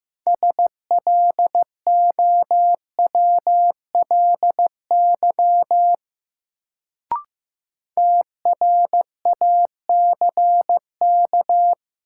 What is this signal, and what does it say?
Morse code, heard as a keyed tone, about 15 wpm, 700 Hz. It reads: SLOWLY TRACK
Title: Sentences from Top 1100 Words - New Word Drill Alt - 15wpm